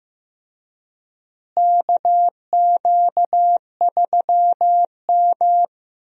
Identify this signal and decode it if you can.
Morse code, heard as a keyed tone, about 15 wpm, 700 Hz. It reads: KQ3M
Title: USA Call Signs - 4 Characters - Repeated 3x - 15wpm